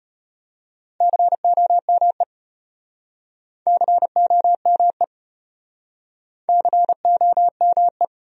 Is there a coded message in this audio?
Yes: Morse code from 1.0 to 8.0 s, heard as a keyed tone, about 36 wpm, 700 Hz. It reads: COME COME COME